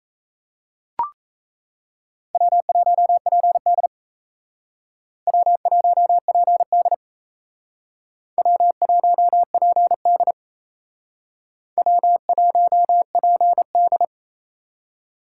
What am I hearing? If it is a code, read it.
Morse code, heard as a keyed tone, about 36 wpm, 700 Hz. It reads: W1PD W1PD W1PD W1PD